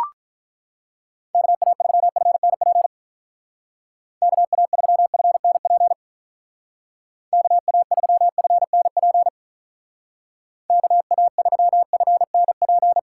Morse code, heard as a keyed tone, about 43 wpm, 700 Hz. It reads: KA3FNP KA3FNP KA3FNP KA3FNP
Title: Speed Racing - USA 2x3 Call Signs 35wpm